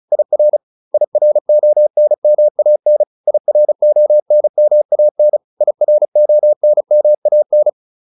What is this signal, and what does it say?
Morse code, heard as a keyed tone, about 34 wpm, 600 Hz. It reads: IR IRONMAN IRONMAN IRONMAN